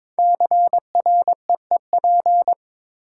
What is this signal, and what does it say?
Morse code, heard as a keyed tone, about 22 wpm, 700 Hz. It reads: CREEP
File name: 5 Letter Words - First 3 Letters Given - 22wpm